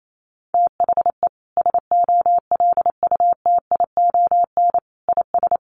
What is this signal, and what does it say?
Morse code, heard as a keyed tone, about 28 wpm, 700 Hz. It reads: THE SOLUTION IS
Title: Sentences from Top 1300 Words 28wpm